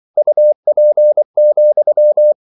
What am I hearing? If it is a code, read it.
Morse code, heard as a keyed tone, about 24 wpm, 600 Hz. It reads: UP,